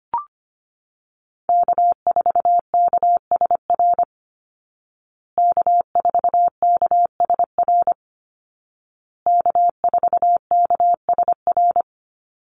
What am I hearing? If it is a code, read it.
Morse code, heard as a keyed tone, about 25 wpm, 700 Hz. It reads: K4KSR K4KSR K4KSR
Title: Call Signs Repeated 2x - 25wpm